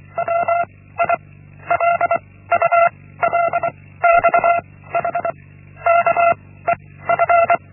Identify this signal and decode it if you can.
Morse code, heard as a keyed tone, about 24 wpm, 680 Hz. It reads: W I L U L X H K E F
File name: Linksmakalnis-1